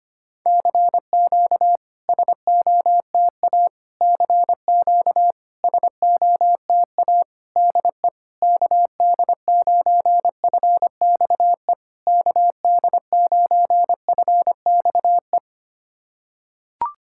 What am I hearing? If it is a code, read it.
Morse code, heard as a keyed tone, about 25 wpm, 690 Hz. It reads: CQ SOTA CQ SOTA DE KD9FXE KD9FXE
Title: SOTA QSO Practice 25wpm